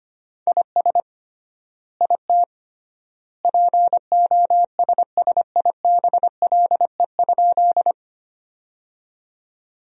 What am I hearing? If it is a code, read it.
Morse code, heard as a keyed tone, about 25 wpm, 700 Hz. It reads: IS IT POSSIBLE?
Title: 3x Word-Spacing - Sentences from Top 500 Words - 25wpm